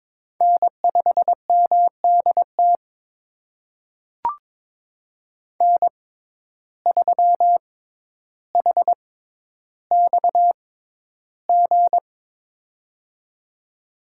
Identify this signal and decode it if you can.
Morse code, heard as a keyed tone, about 22 wpm, spaced as for 11 wpm, 700 Hz. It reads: N5MDT N3HXG
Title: USA Call Signs - 5 Characters - 6x Character Spacing - 22wpm